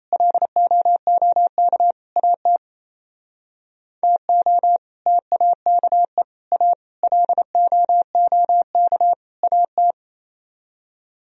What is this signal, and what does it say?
Morse code, heard as a keyed tone, about 33 wpm, 700 Hz. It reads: LOOK AT TO TAKE A LOOK AT
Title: Speed-Racing - Sets of 5 Words 28wpm